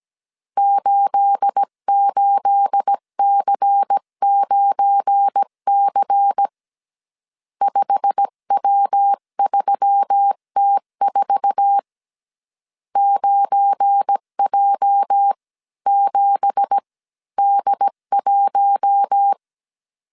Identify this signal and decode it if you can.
Morse code, heard as a keyed tone, about 17 wpm, spaced as for 15 wpm, 800 Hz. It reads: 88C9C 5W3T4 9J7D1
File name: CW / DET / 1 9 1500C S B 16wpm 2856